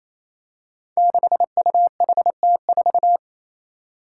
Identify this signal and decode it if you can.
Morse code, heard as a keyed tone, about 27 wpm, 700 Hz. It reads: 6UHT4
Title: Five Challenging Letter-Number Combinations 28wpm